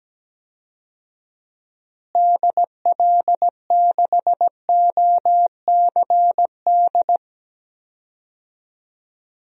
Morse code, heard as a keyed tone, about 17 wpm, 700 Hz. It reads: DL6OCD
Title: DXCC-100 Call Signs - 6 Characters - Repeated 3x - 17wpm